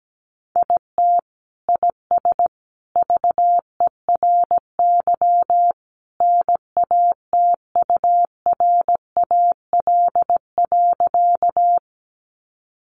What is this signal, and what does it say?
Morse code, heard as a keyed tone, about 17 wpm, 700 Hz. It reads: IT IS VERY NATURAL.
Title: Sentences from Top 700 Words 17wpm